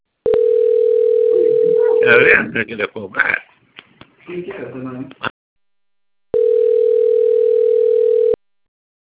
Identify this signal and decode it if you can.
Ringback tone (North America), 440 + 480 Hz, with speech.
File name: rg-1600-unknown- - - .22858